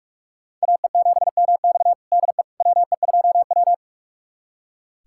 Morse code, heard as a keyed tone, about 45 wpm, 700 Hz. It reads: AE7MX DE WE2W
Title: USA Callsign DE Callsign 45wpm